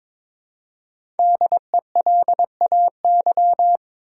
Morse code, heard as a keyed tone, about 22 wpm, 700 Hz. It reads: DELAY
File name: 5 Letter Words - First 3 Letters Given - 22wpm